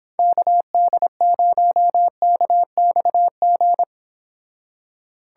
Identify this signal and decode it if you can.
Morse code, heard as a keyed tone, about 31 wpm, 700 Hz. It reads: KD0KXG